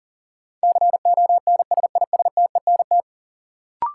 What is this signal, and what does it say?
Morse code, heard as a keyed tone, about 39 wpm, 700 Hz. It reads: CONSISTENT
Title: Sentences from Top 1500 Words - New Word Drill - 40wpm